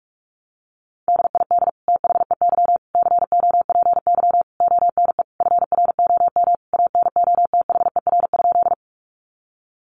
Morse code, heard as a keyed tone, about 44 wpm, 700 Hz. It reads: DID THEY COPY ONE FROM ANOTHER?
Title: Sentences from Top 1400 Words 45wpm